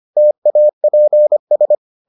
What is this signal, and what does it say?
Morse code, heard as a keyed tone, about 25 wpm, 600 Hz. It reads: TAPS